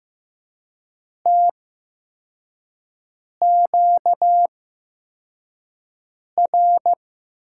Morse code, heard as a keyed tone, about 15 wpm, spaced as for 4 wpm, 700 Hz. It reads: TQR
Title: USA Call Signs - 6 Characters - 8x Character Spacing - 15wpm